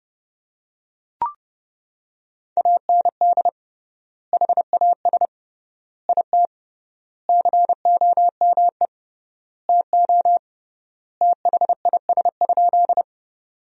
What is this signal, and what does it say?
Morse code, heard as a keyed tone, about 30 wpm, 700 Hz. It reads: AND HAS IT COME TO THIS?
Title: 3x Word-Spacing - Sentences from Top 100 Words - 30wpm